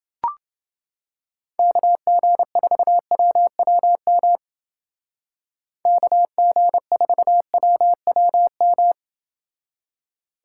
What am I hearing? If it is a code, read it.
Morse code, heard as a keyed tone, about 28 wpm, 700 Hz. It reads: KG4WWM KG4WWM